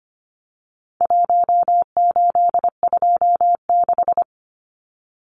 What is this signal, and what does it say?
Morse code, heard as a keyed tone, about 25 wpm, 700 Hz. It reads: 1826
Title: Numbers - Contest Style - 4-Digit 25wpm